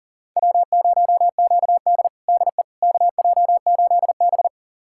Morse code, heard as a keyed tone, about 37 wpm, 690 Hz. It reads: W0QD DE KJ8B
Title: USA Callsign DE Callsign - 4 Characters - 40wpm